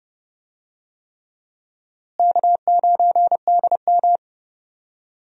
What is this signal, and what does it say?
Morse code, heard as a keyed tone, about 29 wpm, 700 Hz. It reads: K9DM
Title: USA Call Signs - 4 Characters - Repeated 3x - 30wpm